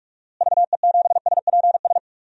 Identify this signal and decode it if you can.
Morse code, heard as a keyed tone, about 42 wpm, 700 Hz. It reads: VE7SPS
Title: DXCC-100 Call Signs - 6 Characters - Repeated 2x - 45wpm